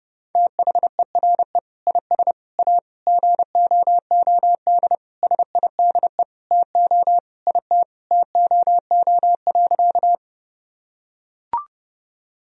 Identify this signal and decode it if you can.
Morse code, heard as a keyed tone, about 30 wpm, 700 Hz. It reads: THERE IS A GOOD SIDE TO IT TOO.
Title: Sentences from Top 1000 Words - Review Part 1 - 30wpm